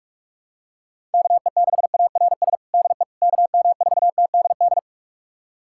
Morse code, heard as a keyed tone, about 44 wpm, 700 Hz. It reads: KE6ARS DE KM4TDD
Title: USA Callsign DE Callsign - 6 Characters - 45wpm